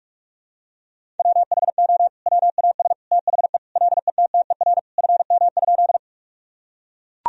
Morse code, heard as a keyed tone, about 41 wpm, 700 Hz. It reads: WHO WAS THE LETTER FM?
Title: Sentences from Top 300 Words and Abbreviations 45wpm